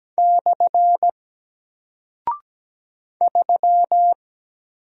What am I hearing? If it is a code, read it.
Morse code, heard as a keyed tone, about 17 wpm, 700 Hz. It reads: / 3
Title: Single Character (Letter:Number:Punctuation:Prosign) Alt 17wpm